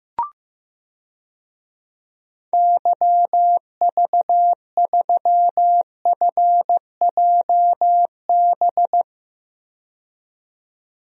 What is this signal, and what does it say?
Morse code, heard as a keyed tone, about 15 wpm, 700 Hz. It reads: YV3FJB
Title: DXCC-100 Call Signs - 6 Characters - 15wpm